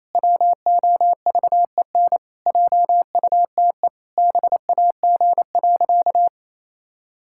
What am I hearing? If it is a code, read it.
Morse code, heard as a keyed tone, about 28 wpm, 700 Hz. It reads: WOVEN JUTE BAG.
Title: Pangram Challenge v2 - 28wpm